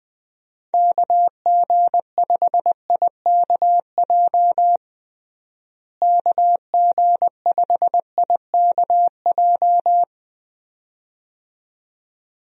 Morse code, heard as a keyed tone, about 20 wpm, 700 Hz. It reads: KG5IKJ KG5IKJ
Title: USA Call Signs - 6 Characters - Repeated 2x - 20wpm